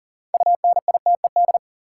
Morse code, heard as a keyed tone, about 40 wpm, 700 Hz. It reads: UNITED
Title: Sentences from Top 600 Words - New Word Drill - 40wpm